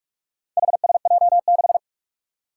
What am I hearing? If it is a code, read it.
Morse code, heard as a keyed tone, about 44 wpm, 700 Hz. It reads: HSJ6
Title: Four Challenging Letter-Number Combinations 45wpm